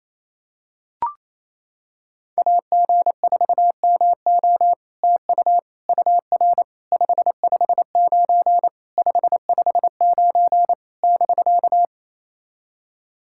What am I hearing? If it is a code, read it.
Morse code, heard as a keyed tone, about 28 wpm, 700 Hz. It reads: AG4MO TU UR 559 559 <BK>
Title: SOTA QSO Practice 28wpm